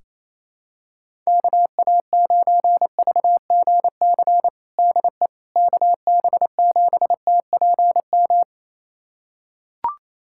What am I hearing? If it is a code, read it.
Morse code, heard as a keyed tone, about 28 wpm, 700 Hz. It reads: KA9VGC DE KB7TPM
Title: USA Callsign DE Callsign - 6 Characters - 28wpm